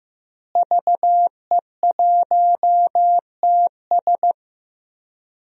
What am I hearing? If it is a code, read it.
Morse code, heard as a keyed tone, about 16 wpm, 700 Hz. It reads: VE1TS
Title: DXCC-100 Call Signs - 5 Characters - Repeated 3x - 15wpm